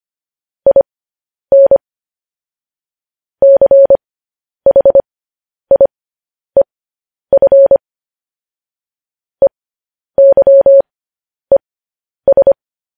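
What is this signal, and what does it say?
Morse code, heard as a keyed tone, about 25 wpm, spaced as for 10 wpm, 550 Hz. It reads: IN CHIEF EYES
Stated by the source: Morse code news headlines broadcast at 10 wpm.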